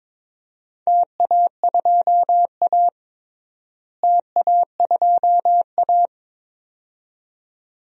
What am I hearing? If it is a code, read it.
Morse code, heard as a keyed tone, about 21 wpm, 700 Hz. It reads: TA2A TA2A
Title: DXCC-100 Call Signs - 4 Characters - Repeated 2x - 22wpm